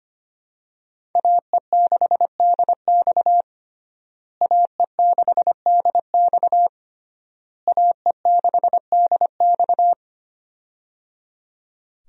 Morse code, heard as a keyed tone, about 25 wpm, 700 Hz. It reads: AE6DX AE6DX AE6DX
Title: USA Call Signs - 5 Characters - Repeated 3x - 25wpm